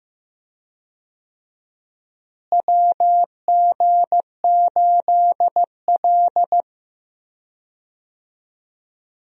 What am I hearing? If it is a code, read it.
Morse code, heard as a keyed tone, about 15 wpm, 700 Hz. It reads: WG8L
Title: USA Call Signs - 4 Characters - Repeated 3x - 15wpm